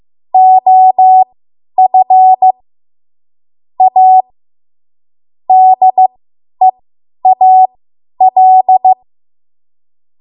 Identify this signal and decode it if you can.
Morse code, heard as a keyed tone, about 15 wpm, spaced as for 10 wpm, 750 Hz. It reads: OF A DEAL